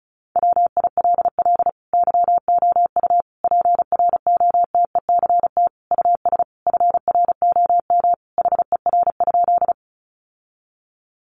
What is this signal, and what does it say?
Morse code, heard as a keyed tone, about 34 wpm, 700 Hz. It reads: WILL YOU PROTECT US FROM HER?
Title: Sentences from Top 900 Words 35wpm